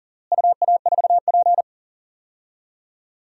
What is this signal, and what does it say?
Morse code, heard as a keyed tone, about 37 wpm, 700 Hz. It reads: UA4P
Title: DXCC-100 Call Signs - 4 Characters - Repeated 2x - 40wpm